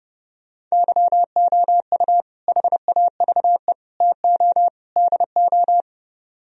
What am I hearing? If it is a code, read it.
Morse code, heard as a keyed tone, about 38 wpm, 700 Hz. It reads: YOU HAVE TO DO